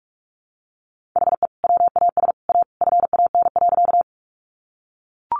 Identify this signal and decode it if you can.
Morse code, heard as a keyed tone, about 44 wpm, 700 Hz. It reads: HE WAS A FAN.
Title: Sentences from Top 1300 Words 45wpm